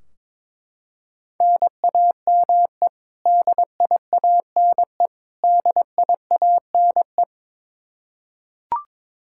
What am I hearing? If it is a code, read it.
Morse code, heard as a keyed tone, about 22 wpm, 700 Hz. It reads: NAME DIANE DIANE
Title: Practice QSO - Exchange 1 - Beginning - 22wpm